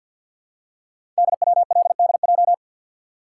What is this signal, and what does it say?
Morse code, heard as a keyed tone, about 51 wpm, 700 Hz. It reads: DWLDJ
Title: Code Groups - 5 Letters - 50wpm